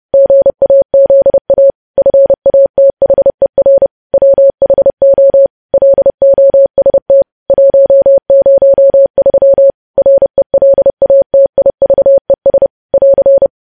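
Morse code, heard as a keyed tone, about 30 wpm, 550 Hz. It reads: GAZA FATHER WHO LOST 103 RELATIVES <AR>
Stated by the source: News headlines sent in Morse code at 30.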